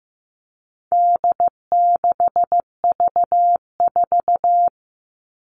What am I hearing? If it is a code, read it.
Morse code, heard as a keyed tone, about 15 wpm, 700 Hz. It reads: D6V4
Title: All Cut Numbers - 4-Digit 15wpm